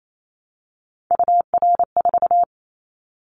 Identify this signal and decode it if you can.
Morse code, heard as a keyed tone, about 28 wpm, 700 Hz. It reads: UR4
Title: Callsign Prefixes for Top 25 DXCC - Repeated 3x - 28wpm